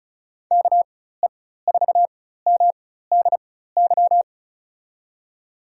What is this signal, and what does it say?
Morse code, heard as a keyed tone, about 36 wpm, spaced as for 16 wpm, 700 Hz. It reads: KE4MDY
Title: USA Call Signs - 6 Characters - 4x Character Spacing - 35wpm